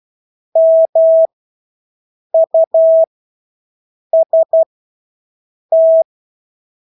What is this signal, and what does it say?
Morse code, heard as a keyed tone, about 12 wpm, spaced as for 6 wpm, 650 Hz. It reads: MUST
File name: Engelske ord 12-6 wpm